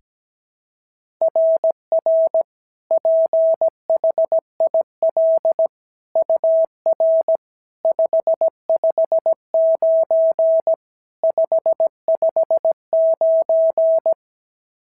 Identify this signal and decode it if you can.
Morse code, heard as a keyed tone, about 16 wpm, 650 Hz. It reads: RR PHIL UR 559 559
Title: SOTA QSO Practice 17wpm